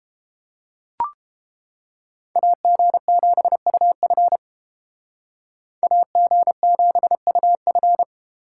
Morse code, heard as a keyed tone, about 30 wpm, 700 Hz. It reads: AG7UF AG7UF